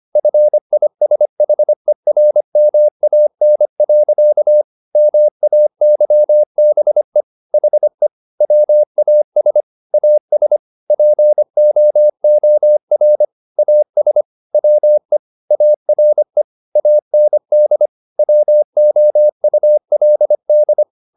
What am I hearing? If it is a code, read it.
Morse code, heard as a keyed tone, about 25 wpm, 600 Hz. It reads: FISHERMAN. MAYBE HE WAS AS POOR AS WE ARE AND WOULD
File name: the old man and the sea part 1 25wpm